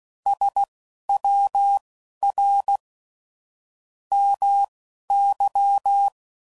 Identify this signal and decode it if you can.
Morse code, heard as a keyed tone, about 16 wpm, spaced as for 11 wpm, 800 Hz. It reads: SWR MY
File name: CW Code Groups @15WPM/QSOs/WD0FBT Spaced